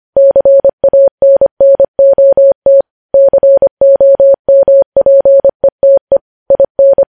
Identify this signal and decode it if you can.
Morse code, heard as a keyed tone, about 25 wpm, 550 Hz. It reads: CANNOT COMPETE IN
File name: podcast / News headlines-1x-25x25